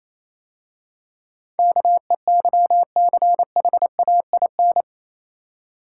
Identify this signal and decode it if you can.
Morse code, heard as a keyed tone, about 26 wpm, 690 Hz. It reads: KEYCHAIN
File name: Easy Words - Speed-Racing with Hints - 22wpm